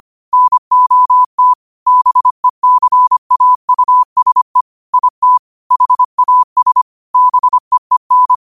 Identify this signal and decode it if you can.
Morse code, heard as a keyed tone, about 25 wpm, 1000 Hz. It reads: NOT BECAUSE IT HAS BEEN